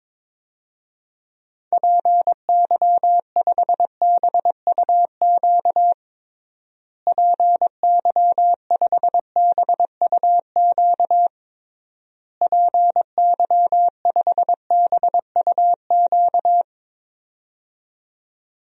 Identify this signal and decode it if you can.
Morse code, heard as a keyed tone, about 22 wpm, 700 Hz. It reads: PY5BUQ PY5BUQ PY5BUQ
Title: DXCC-100 Call Signs - 6 Characters - Repeated 3x - 22wpm